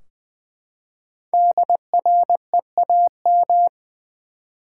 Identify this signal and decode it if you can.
Morse code, heard as a keyed tone, about 21 wpm, 700 Hz. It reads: DREAM
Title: Five Letter Words - 20wpm